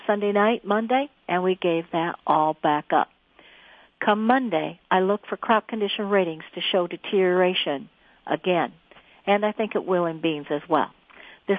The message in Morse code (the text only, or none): none